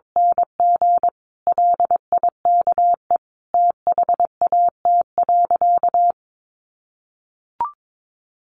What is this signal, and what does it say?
Morse code, heard as a keyed tone, about 22 wpm, 700 Hz. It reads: NG LIKE THAT.
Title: A Sound Of Thunder 22wpm